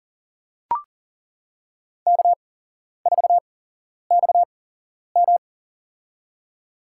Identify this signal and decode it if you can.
Morse code, heard as a keyed tone, about 40 wpm, spaced as for 11 wpm, 700 Hz. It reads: K4XM
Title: USA Call Signs - 4 Characters - 8x Character Spacing - 40wpm